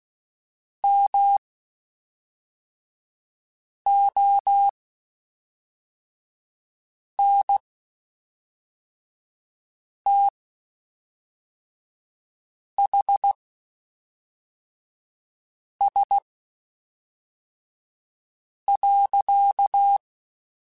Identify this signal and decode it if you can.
Morse code, heard as a keyed tone, about 16 wpm, spaced as for 3 wpm, 800 Hz. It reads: MONTHS.
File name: QSOs / QSOs-mp3s / WD0BGP 1@5